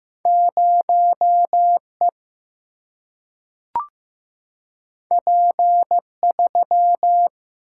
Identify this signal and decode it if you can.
Morse code, heard as a keyed tone, about 15 wpm, 700 Hz. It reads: 0 E P 3
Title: Single-Letter Single-Number 15wpm